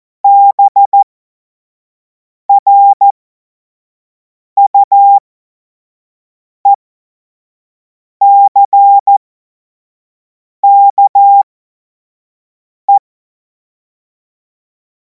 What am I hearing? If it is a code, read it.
Morse code, heard as a keyed tone, about 14 wpm, spaced as for 5 wpm, 800 Hz. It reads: BRUECKE
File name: Morsekurs / Witze / Witz 19. Mai 2026 25 F70